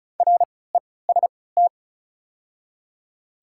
Morse code, heard as a keyed tone, about 36 wpm, spaced as for 20 wpm, 690 Hz. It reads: REST
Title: 3x Character Spacing - Top 500 Words - 35wpm